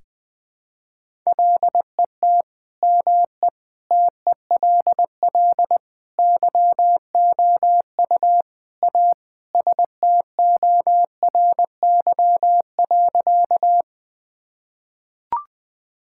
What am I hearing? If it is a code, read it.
Morse code, heard as a keyed tone, about 20 wpm, 700 Hz. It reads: LET ME TELL YOU A STORY.
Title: Sentences with 4-Word Phrases - 20wpm